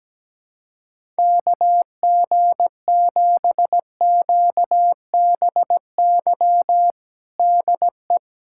Morse code, heard as a keyed tone, about 17 wpm, 700 Hz. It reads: KG7QBY DE
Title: USA Callsign DE Callsign - 6 Characters - 17wpm